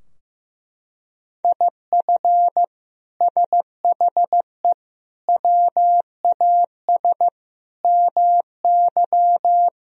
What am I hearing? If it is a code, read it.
Morse code, heard as a keyed tone, about 15 wpm, 700 Hz. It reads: IF SHE WAS MY
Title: Sentences from Top 1000 Words - Review Part 2 - 15wpm